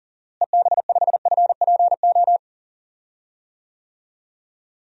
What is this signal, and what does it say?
Morse code, heard as a keyed tone, about 40 wpm, 700 Hz. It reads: EB5FPO
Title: DXCC-100 Call Signs - 6 Characters - Repeated 2x - 40wpm